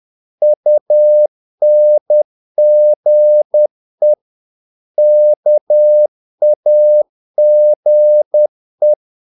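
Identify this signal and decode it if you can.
Morse code, heard as a keyed tone, about 10 wpm, 600 Hz. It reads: UNGE KAGE
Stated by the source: Morse practice of short Danish words at 10 wpm, from CW med Gnister.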